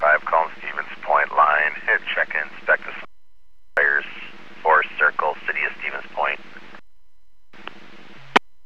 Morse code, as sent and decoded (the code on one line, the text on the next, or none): none